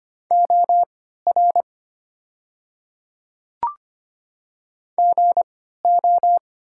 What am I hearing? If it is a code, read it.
Morse code, heard as a keyed tone, about 25 wpm, spaced as for 14 wpm, 700 Hz. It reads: OR GO
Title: 3x Character Spacing - Top 100 Words - 25wpm